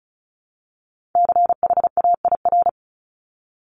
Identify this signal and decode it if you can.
Morse code, heard as a keyed tone, about 34 wpm, 700 Hz. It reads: CHAIR
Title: Top 1751-2000 Words 35wpm